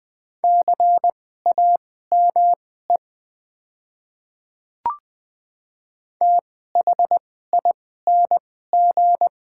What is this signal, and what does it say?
Morse code, heard as a keyed tone, about 21 wpm, spaced as for 15 wpm, 690 Hz. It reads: CAME THING
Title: 2x Character Spacing - Top 200 Words - 20wpm